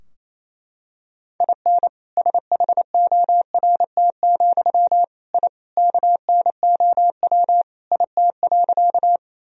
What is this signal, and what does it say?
Morse code, heard as a keyed tone, about 28 wpm, 700 Hz. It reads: IN SHORT, I KNOW IT.
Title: Sentences from Top 1000 Words - Review Part 2 - 28wpm